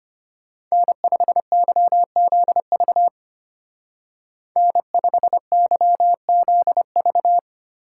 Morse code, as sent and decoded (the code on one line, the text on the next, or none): -. ..... -.-- --.. ...- / -. ..... -.-- --.. ...-
N5YZV N5YZV